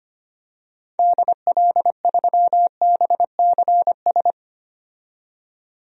Morse code, heard as a keyed tone, about 25 wpm, 700 Hz. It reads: DL3BCS
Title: DXCC-100 Call Signs - 6 Characters - 25wpm